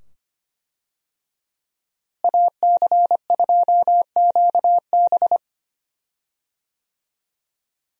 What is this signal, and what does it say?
Morse code, heard as a keyed tone, about 25 wpm, 700 Hz. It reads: AC2QB
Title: USA Call Signs - 5 Characters - Repeated 2x - 25wpm